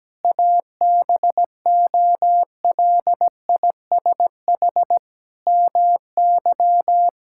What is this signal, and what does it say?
Morse code, heard as a keyed tone, about 17 wpm, 700 Hz. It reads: ABOLISH MY
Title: Pangram Challenge v2 - 17wpm